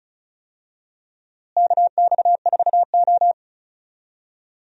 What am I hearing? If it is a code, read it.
Morse code, heard as a keyed tone, about 36 wpm, 700 Hz. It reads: KX4O
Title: USA Call Signs - 4 Characters - Repeated 2x - 35wpm